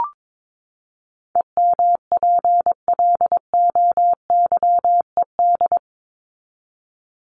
Morse code, heard as a keyed tone, about 22 wpm, 700 Hz. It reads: EMPLOYED
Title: ED Suffix - Top 100 Words with Base Words 22wpm